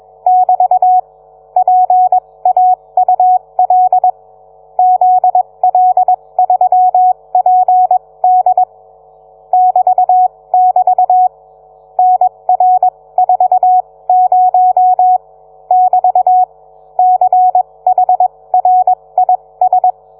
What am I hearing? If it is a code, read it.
Morse code, heard as a keyed tone, about 21 wpm, spaced as for 18 wpm, 740 Hz. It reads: <BT> PAUL ZL3PD <BT><BT> NR40 <BT> CHRIS